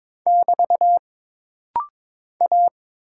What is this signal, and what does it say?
Morse code, heard as a keyed tone, about 22 wpm, 700 Hz. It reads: <BT> A
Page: This rapid-fire timing is geared to push your ICR (Instant Character Recognition) skills.